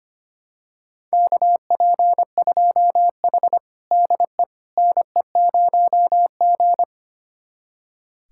Morse code, heard as a keyed tone, about 25 wpm, 700 Hz. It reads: KP2H DE NE0G
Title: USA Callsign DE Callsign - 4 Characters - 25wpm